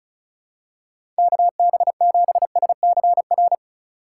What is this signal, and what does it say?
Morse code, heard as a keyed tone, about 36 wpm, 700 Hz. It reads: KB7SCR
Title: USA Call Signs - 6 Characters - 4x Character Spacing - 35wpm